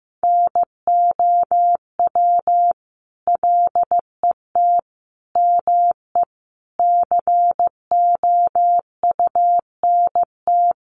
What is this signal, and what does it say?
Morse code, heard as a keyed tone, about 14 wpm, 700 Hz. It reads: NOW LET ME COUNT
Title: Sentences from Top 1500 Words 15wpm